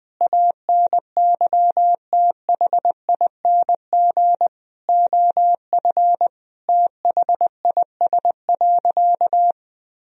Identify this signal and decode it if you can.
Morse code, heard as a keyed tone, about 21 wpm, 700 Hz. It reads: ANYTHING OF THIS.
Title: Sentences from Top 1000 Words - Review Part 3 - 20wpm